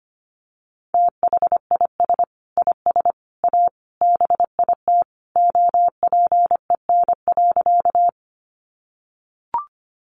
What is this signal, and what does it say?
Morse code, heard as a keyed tone, about 25 wpm, 700 Hz. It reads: THIS IS A BIT OPEN.
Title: Sentences from Top 900 Words 25wpm